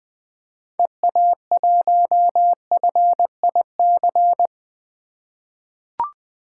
Morse code, heard as a keyed tone, about 20 wpm, 700 Hz. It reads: EA1FIC
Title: 2023 POTA Hunter Call Signs - 20wpm